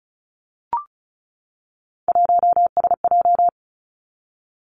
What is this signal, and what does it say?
Morse code, heard as a keyed tone, about 35 wpm, 700 Hz. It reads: 1SJ
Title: Three Challenging Letter-Number Combinations 35wpm